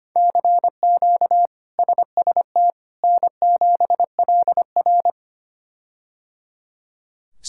Morse code, heard as a keyed tone, about 25 wpm, 700 Hz. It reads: CQ SST N7LR
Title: K1USN SST QSO Practice - 25wpm